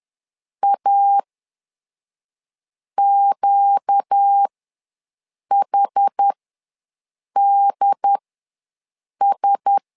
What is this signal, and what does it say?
Morse code, heard as a keyed tone, about 10 wpm, spaced as for 5 wpm, 800 Hz. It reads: AQHDS
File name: LEZIONI / 4 Q1S ws 10wpm 2208